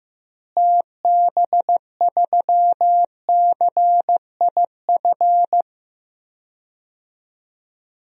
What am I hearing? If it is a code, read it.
Morse code, heard as a keyed tone, about 15 wpm, 700 Hz. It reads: TB3CIF
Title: DXCC-100 Call Signs - 6 Characters - Repeated 2x - 15wpm